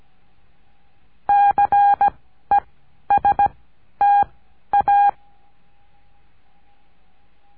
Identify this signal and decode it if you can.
Morse code, heard as a keyed tone, about 17 wpm, 810 Hz. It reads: C E S T A
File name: cursos / CURSO DE TELEGRAFIA / Telegrafia - Trilha 14